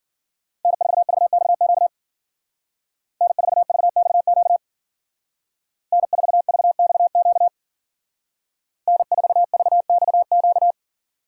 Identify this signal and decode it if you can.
Morse code, heard as a keyed tone, about 49 wpm, 700 Hz. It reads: N4VXQ N4VXQ N4VXQ N4VXQ
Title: Speed Racing - USA 1x3 Call Signs 40wpm